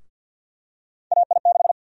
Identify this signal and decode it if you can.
Morse code, heard as a keyed tone, about 51 wpm, 700 Hz. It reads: AI6
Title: USA Call Sign Prefixes - Repeated 2x - 50wpm